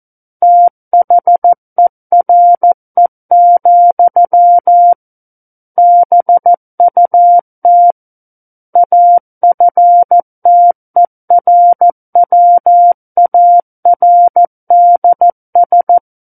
Morse code, heard as a keyed tone, about 14 wpm, 700 Hz. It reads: THERE, BUT AFTERWARDS
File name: Book One / 05-Chapter 5 - 14 WPM